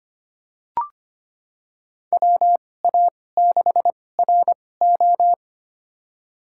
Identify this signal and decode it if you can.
Morse code, heard as a keyed tone, about 25 wpm, spaced as for 18 wpm, 700 Hz. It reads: WA6RO
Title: USA Call Signs - 5 Characters - 2x Character Spacing - 25wpm